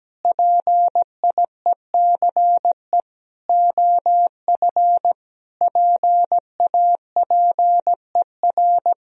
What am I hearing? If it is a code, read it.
Morse code, heard as a keyed tone, about 17 wpm, 700 Hz. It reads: PIECE OF PAPER
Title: Sentences from Top 400 Words and Abbreviations 17wpm